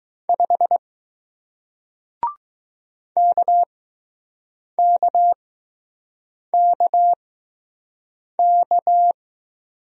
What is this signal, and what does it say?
Morse code, heard as a keyed tone, about 20 wpm, spaced as for 6 wpm, 700 Hz. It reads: 5 KKKK